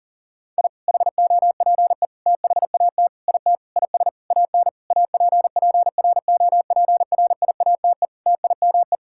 Morse code, heard as a keyed tone, about 37 wpm, 700 Hz. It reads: I HOPE THAT IT IS AN APPROPRIATE TIME
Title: Sentences from Top 1000 Words - Review Part 5 - 40wpm